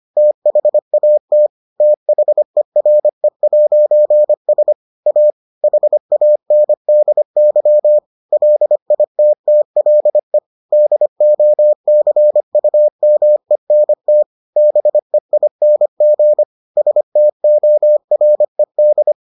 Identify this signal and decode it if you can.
Morse code, heard as a keyed tone, about 25 wpm, 600 Hz. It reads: THAT THERE'S A HANDY LITTLE DOCUMENT BEING STORED